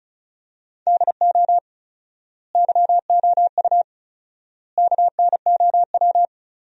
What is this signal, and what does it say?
Morse code, heard as a keyed tone, about 36 wpm, 700 Hz. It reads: DO YOU KNOW
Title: 4x Word-Spacing - Sentences from Top 200 Words - 35wpm